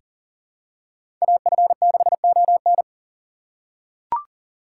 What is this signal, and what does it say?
Morse code, heard as a keyed tone, about 40 wpm, 700 Hz. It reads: AF6ON
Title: USA Call Signs - 5 Characters - 2x Character Spacing - 40wpm